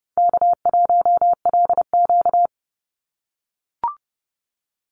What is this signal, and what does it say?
Morse code, heard as a keyed tone, about 30 wpm, 700 Hz. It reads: K1LQ
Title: USA Call Signs 30wpm